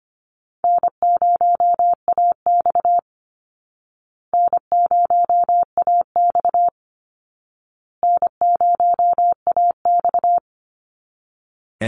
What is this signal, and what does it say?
Morse code, heard as a keyed tone, about 25 wpm, 700 Hz. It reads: N0AX N0AX N0AX
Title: Call Signs Repeated 2x - 25wpm